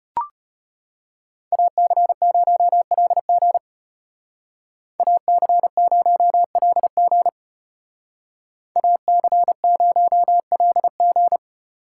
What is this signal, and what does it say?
Morse code, heard as a keyed tone, about 34 wpm, 700 Hz. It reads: AC0LG AC0LG AC0LG